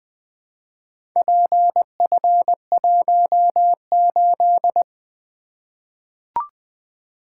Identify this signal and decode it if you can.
Morse code, heard as a keyed tone, about 20 wpm, 700 Hz. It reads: PF18
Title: Four Letter-Number Combinations - 20wpm